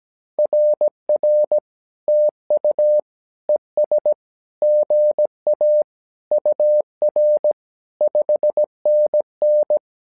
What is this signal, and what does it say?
Morse code, heard as a keyed tone, about 17 wpm, 600 Hz. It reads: RR TU ES GA UR 5NN
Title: SOTA QSO Practice 17wpm